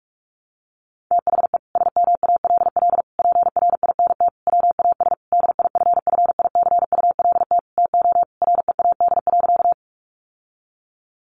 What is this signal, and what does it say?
Morse code, heard as a keyed tone, about 44 wpm, 700 Hz. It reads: THE SMALL PRINT WAS DIFFICULT TO READ.
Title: Sentences from Top 1400 Words 45wpm